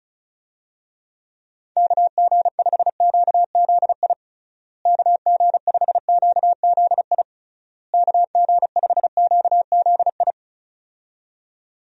Morse code, heard as a keyed tone, about 36 wpm, 700 Hz. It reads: KG5QZI KG5QZI KG5QZI
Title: USA Call Signs - 6 Characters - Repeated 3x - 35wpm